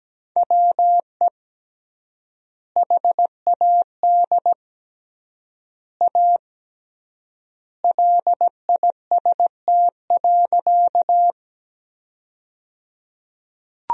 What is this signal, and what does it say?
Morse code, heard as a keyed tone, about 17 wpm, 700 Hz. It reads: WE HAD A LIST.
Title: 3x Word-Spacing - Sentences from Top 400 Words - 17wpm